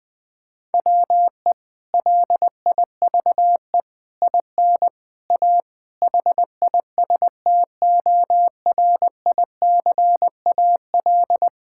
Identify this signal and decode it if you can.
Morse code, heard as a keyed tone, about 19 wpm, 700 Hz. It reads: WE LIVE IN A HISTORICAL
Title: Sentences from Top 1000 Words - Review Part 5 - 20wpm